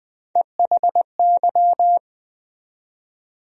Morse code, heard as a keyed tone, about 20 wpm, 700 Hz. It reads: EHY
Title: USA Call Sign Postfixes - 3 Letters 20wpm